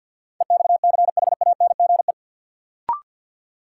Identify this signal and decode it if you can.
Morse code, heard as a keyed tone, about 53 wpm, 700 Hz. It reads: EXCHANGE
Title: Sentences from Top 1000 Words - Review Part 6 - New Word Drill - 50wpm